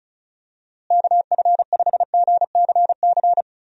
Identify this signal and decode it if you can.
Morse code, heard as a keyed tone, about 36 wpm, 700 Hz. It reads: KF5GCC
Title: USA Call Signs - 6 Characters - Repeated 2x - 35wpm